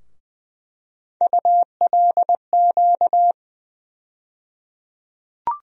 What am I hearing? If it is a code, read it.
Morse code, heard as a keyed tone, about 21 wpm, 700 Hz. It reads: ULQ
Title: USA Call Sign Postfixes - 3 Letters 20wpm